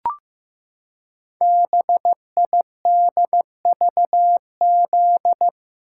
Morse code, heard as a keyed tone, about 15 wpm, 700 Hz. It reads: BIDVZ
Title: Code Groups - 5 Letter:Numbers - 15wpm